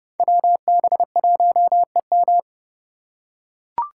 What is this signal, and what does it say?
Morse code, heard as a keyed tone, about 28 wpm, 700 Hz. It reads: WB1EM
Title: Code Groups - 5 Letter:Numbers - 30wpm